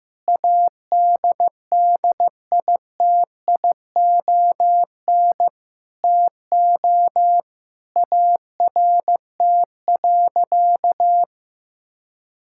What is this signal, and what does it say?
Morse code, heard as a keyed tone, about 16 wpm, 700 Hz. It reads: ADDITION TO ART.
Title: Sentences from Top 1000 Words - Review Part 6 - 15wpm